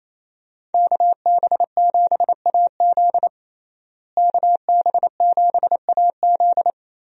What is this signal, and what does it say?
Morse code, heard as a keyed tone, about 28 wpm, 700 Hz. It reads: KB7AZ KB7AZ
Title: USA Call Signs - 5 Characters - Repeated 3x - 28wpm